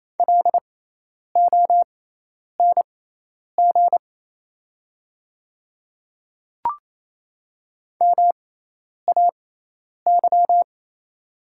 Morse code, heard as a keyed tone, about 28 wpm, spaced as for 10 wpm, 700 Hz. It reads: LONG MAY
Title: 6x Character Spacing - Top 100 Words - 28wpm